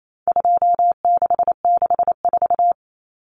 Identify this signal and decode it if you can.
Morse code, heard as a keyed tone, about 28 wpm, 700 Hz. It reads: 2664
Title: Common Cut Numbers - 4-Digit 28wpm